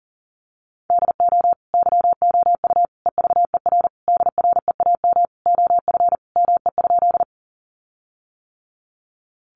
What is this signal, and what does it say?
Morse code, heard as a keyed tone, about 40 wpm, 700 Hz. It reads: DO YOU EVER DREAM OF ME?
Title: Sentences from Top 1500 Words 40wpm